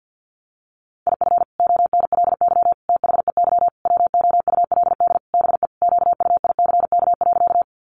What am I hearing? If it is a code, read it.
Morse code, heard as a keyed tone, about 50 wpm, 700 Hz. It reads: IF ONLY THEY WOULD BE QUICK.
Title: Sentences from Top 500 Words 50wpm